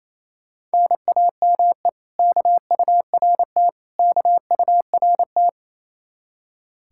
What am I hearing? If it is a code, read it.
Morse code, heard as a keyed tone, about 37 wpm, 700 Hz. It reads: NAME KURT KURT